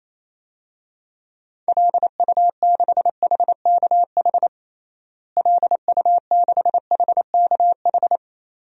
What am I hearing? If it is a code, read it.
Morse code, heard as a keyed tone, about 28 wpm, 700 Hz. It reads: LU6HKH LU6HKH
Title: DXCC-100 Call Signs - 6 Characters - Repeated 3x - 28wpm